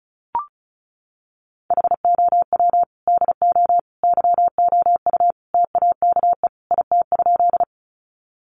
Morse code, heard as a keyed tone, about 34 wpm, 700 Hz. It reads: HOW DO YOU TAKE IT?
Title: Sentences from Top 200 Words 35wpm